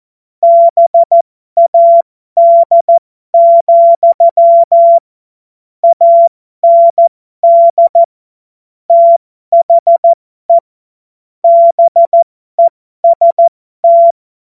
Morse code, heard as a keyed tone, about 14 wpm, spaced as for 12 wpm, 680 Hz. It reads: BAD, AND THE BEST
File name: Morsekurs / Witze